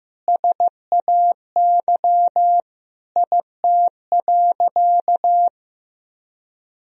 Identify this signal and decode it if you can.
Morse code, heard as a keyed tone, about 15 wpm, 700 Hz. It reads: SAY IT.
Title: Sentences from Top 400 Words and Abbreviations 15wpm